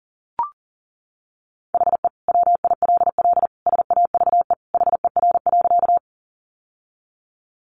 Morse code, heard as a keyed tone, about 40 wpm, 700 Hz. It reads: HE WILL SAVE HER.
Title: Sentences from Top 1100 Words 40wpm